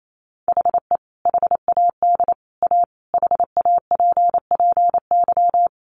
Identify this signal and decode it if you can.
Morse code, heard as a keyed tone, about 28 wpm, 700 Hz. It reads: HE HAD A HAPPY
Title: Sentences from Top 1000 Words 28wpm